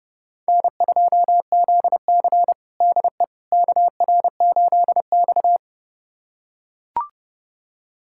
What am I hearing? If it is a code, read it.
Morse code, heard as a keyed tone, about 30 wpm, 700 Hz. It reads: N2ZC DE KR8X
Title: USA Callsign DE Callsign - 4 Characters - 30wpm